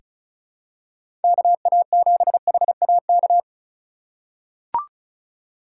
Morse code, heard as a keyed tone, about 36 wpm, 700 Hz. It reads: KA7HAK
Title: USA Call Signs - 6 Characters - 3x Character Spacing - 35wpm